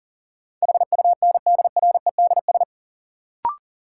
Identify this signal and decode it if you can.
Morse code, heard as a keyed tone, about 40 wpm, 700 Hz. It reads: HUNDREDS
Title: Sets of 4 Words - New Word Drill - 40wpm